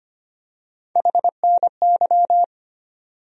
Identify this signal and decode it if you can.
Morse code, heard as a keyed tone, about 25 wpm, 700 Hz. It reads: HNY
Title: USA Call Sign Postfixes - 3 Letters 25wpm